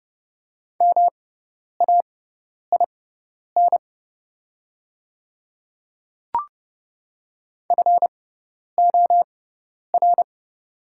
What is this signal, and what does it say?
Morse code, heard as a keyed tone, about 32 wpm, spaced as for 10 wpm, 700 Hz. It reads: MAIN FOR
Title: 6x Character Spacing - Top 300 Words - 30wpm